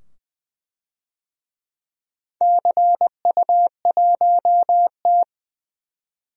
Morse code, heard as a keyed tone, about 20 wpm, 700 Hz. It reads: CU1T
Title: DXCC-100 Call Signs - 4 Characters - 20wpm